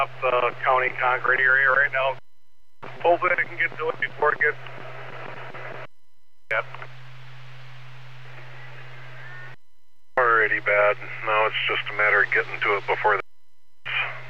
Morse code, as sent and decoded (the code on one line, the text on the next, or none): none